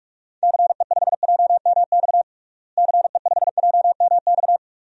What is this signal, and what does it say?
Morse code, heard as a keyed tone, about 46 wpm, 700 Hz. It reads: CE5JMX CE5JMX
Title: DXCC-100 Call Signs - 6 Characters - Repeated 2x - 45wpm